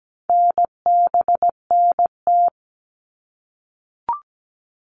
Morse code, heard as a keyed tone, about 17 wpm, 700 Hz. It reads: NBNT
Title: All Cut Numbers - 4-Digit 17wpm